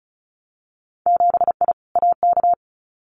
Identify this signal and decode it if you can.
Morse code, heard as a keyed tone, about 36 wpm, 700 Hz. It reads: 7I AK
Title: Winter Field Day Exchanges Encore 35wpm